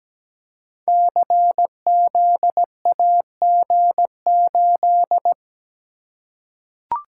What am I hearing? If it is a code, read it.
Morse code, heard as a keyed tone, about 17 wpm, 700 Hz. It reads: CZAG8
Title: Code Groups - 5 Letter:Numbers - 17wpm